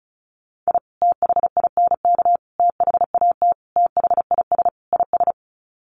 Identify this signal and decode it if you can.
Morse code, heard as a keyed tone, about 34 wpm, 700 Hz. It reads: I THINK THAT THIS IS
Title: Sets of 5 Words Encore2 35wpm